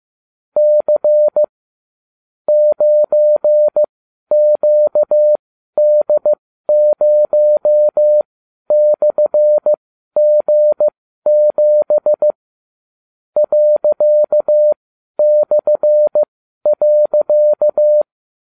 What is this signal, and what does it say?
Morse code, heard as a keyed tone, about 15 wpm, spaced as for 12 wpm, 600 Hz. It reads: C 9QD0/G7 ./.